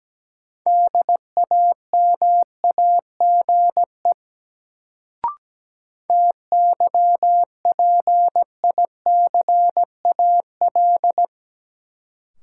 Morse code, heard as a keyed tone, about 17 wpm, 700 Hz. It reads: DAMAGE TYPICAL
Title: Sentences from Top 1100 Words - New Word Drill Alt - 17wpm